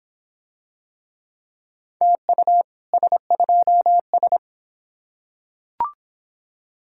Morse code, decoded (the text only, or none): TU S2S